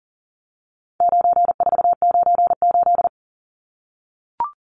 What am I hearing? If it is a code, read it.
Morse code, heard as a keyed tone, about 38 wpm, 700 Hz. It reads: 9498
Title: Numbers - 4-Digit 40wpm